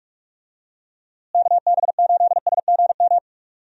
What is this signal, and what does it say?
Morse code, heard as a keyed tone, about 45 wpm, 700 Hz. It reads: KB8SGM
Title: USA Call Signs - 6 Characters - Repeated 3x - 45wpm